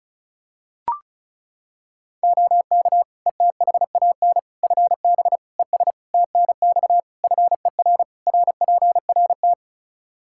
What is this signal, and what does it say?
Morse code, heard as a keyed tone, about 36 wpm, 700 Hz. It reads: OK ETHAN FB ES TNX FER RPRT
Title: Practice QSO - Exchange 2 - Info - 35wpm